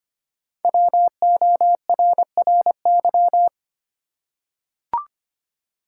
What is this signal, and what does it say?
Morse code, heard as a keyed tone, about 25 wpm, 700 Hz. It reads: WORRY
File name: 5 Letter Words - First 3 Letters Given - 25wpm